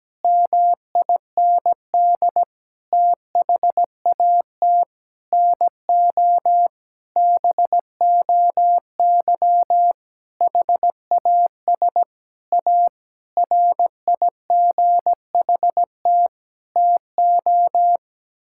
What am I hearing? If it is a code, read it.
Morse code, heard as a keyed tone, about 17 wpm, 700 Hz. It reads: MIND THAT NO BOY HAS A RIGHT TO
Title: Sentences from Top 1000 Words - Review Part 6 - 17wpm